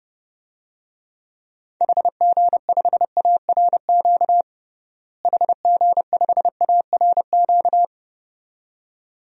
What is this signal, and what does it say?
Morse code, heard as a keyed tone, about 30 wpm, 700 Hz. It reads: HG5ARQ HG5ARQ
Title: DXCC-100 Call Signs - 6 Characters - Repeated 2x - 30wpm